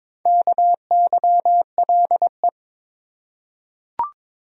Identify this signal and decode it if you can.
Morse code, heard as a keyed tone, about 22 wpm, 700 Hz. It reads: KYLE
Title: USA Common Names - Max 4 Letters - 22wpm